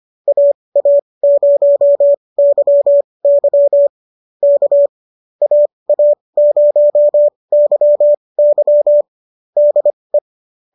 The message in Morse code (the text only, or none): AA0YY K AA0YY DE